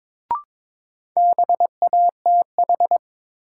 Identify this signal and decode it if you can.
Morse code, heard as a keyed tone, about 22 wpm, 700 Hz. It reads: BATH
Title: Sentences from Top 1600 Words - New Word Drill - 22wpm